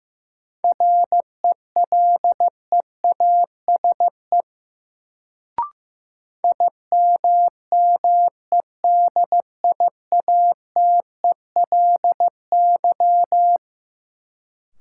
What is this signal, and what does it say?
Morse code, heard as a keyed tone, about 15 wpm, 700 Hz. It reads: RELEASE IMMEDIATELY
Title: Sentences from Top 1100 Words - New Word Drill - 15wpm